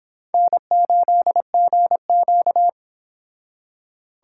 Morse code, heard as a keyed tone, about 30 wpm, 700 Hz. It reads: N8GQ